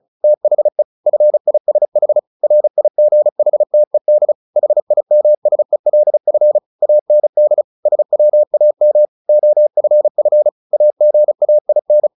Morse code, heard as a keyed tone, about 34 wpm, 600 Hz. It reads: THE FISH RIGHTED HIMSELF AND SWAM OFF AGAIN
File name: the old man and the sea 2 35wpm